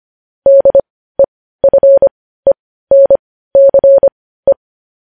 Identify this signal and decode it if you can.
Morse code, heard as a keyed tone, about 25 wpm, spaced as for 15 wpm, 550 Hz. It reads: DEFENCE
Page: news-headlines-in-morse-code-at-15-wpm